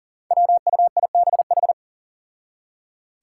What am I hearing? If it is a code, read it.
Morse code, heard as a keyed tone, about 40 wpm, 690 Hz. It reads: WUIBH
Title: Five Challenging Letter-Number Combinations 40wpm